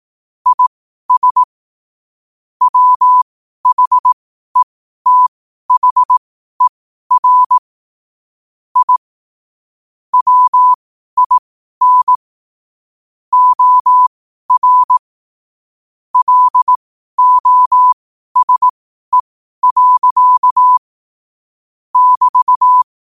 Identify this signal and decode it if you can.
Morse code, heard as a keyed tone, about 18 wpm, spaced as for 12 wpm, 1000 Hz. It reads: IS WHETHER I WIN OR LOSE. <BT>
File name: qotdfiles / 12wpm